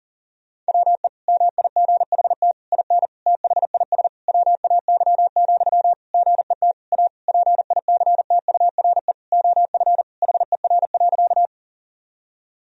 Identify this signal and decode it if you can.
Morse code, heard as a keyed tone, about 37 wpm, 700 Hz. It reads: WE MIGHT IN THIS WAY, GET A PICTURE OF HER.
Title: Sentences from Top 1000 Words - Review Part 1 - 40wpm